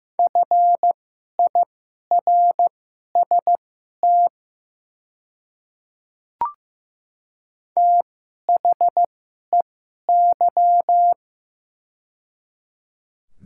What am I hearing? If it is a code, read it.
Morse code, heard as a keyed tone, about 15 wpm, spaced as for 11 wpm, 700 Hz. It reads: FIRST THEY
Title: 2x Character Spacing - Top 100 Words - 15wpm